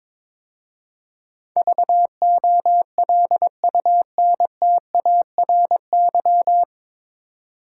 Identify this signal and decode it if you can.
Morse code, heard as a keyed tone, about 23 wpm, 700 Hz. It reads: VOLUNTARY
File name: 9 Letter Words - First 3 Letters Given - 22wpm